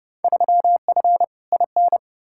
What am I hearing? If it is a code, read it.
Morse code, heard as a keyed tone, about 30 wpm, 700 Hz. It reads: 3F IN
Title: Field Day Exchanges - 30wpm